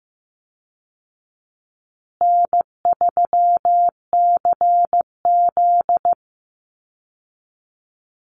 Morse code, heard as a keyed tone, about 15 wpm, 700 Hz. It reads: N3CZ
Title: Call Signs Repeated 2x - 15wpm